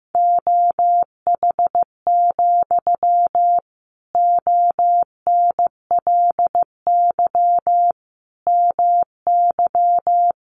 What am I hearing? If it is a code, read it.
Morse code, heard as a keyed tone, about 15 wpm, 700 Hz. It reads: OH, ONLY MY
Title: Sentences from Top 1000 Words 15wpm